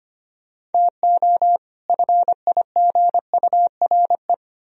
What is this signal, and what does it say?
Morse code, heard as a keyed tone, about 33 wpm, 700 Hz. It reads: TO FIGURE